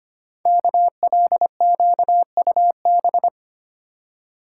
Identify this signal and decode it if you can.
Morse code, heard as a keyed tone, about 25 wpm, 700 Hz. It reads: KLQUB
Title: Code Groups - 5 Letter:Numbers - 25wpm